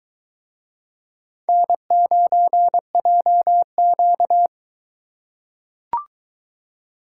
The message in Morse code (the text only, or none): N9JQ